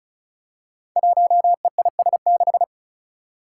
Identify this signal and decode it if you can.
Morse code, heard as a keyed tone, about 36 wpm, 700 Hz. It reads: 1EIS6
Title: Five Challenging Letter-Number Combinations 35wpm